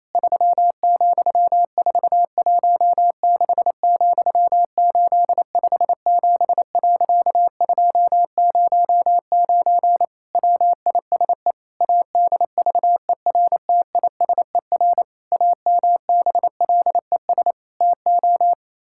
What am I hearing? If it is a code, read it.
Morse code, heard as a keyed tone, about 28 wpm, 700 Hz. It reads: 3,416,857.209 WISE ADVERTISER AMBLES TO
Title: Pangram Challenge v2 - 28wpm